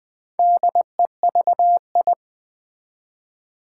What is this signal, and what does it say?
Morse code, heard as a keyed tone, about 21 wpm, 700 Hz. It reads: DEVI
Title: Build A Word - 2 Letters At A Time - 6-Letter Words - 20wpm